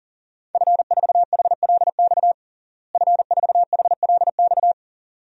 Morse code, heard as a keyed tone, about 40 wpm, 700 Hz. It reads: F4HLX F4HLX
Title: DXCC-100 Call Signs - 5 Characters - Repeated 2x - 40wpm